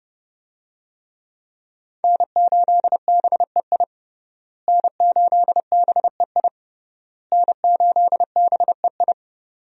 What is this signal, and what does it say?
Morse code, heard as a keyed tone, about 30 wpm, 700 Hz. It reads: N8BEI N8BEI N8BEI
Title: USA Call Signs - 5 Characters - Repeated 3x - 30wpm